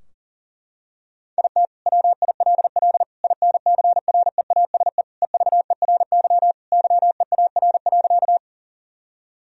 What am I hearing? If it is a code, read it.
Morse code, heard as a keyed tone, about 37 wpm, 700 Hz. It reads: IT WILL INCREASE EVERY YEAR.
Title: Sentences from Top 1000 Words - Review Part 4 - 40wpm